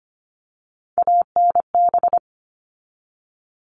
Morse code, heard as a keyed tone, about 25 wpm, 700 Hz. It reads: ANB
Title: All Cut Numbers - 3-Digit 25wpm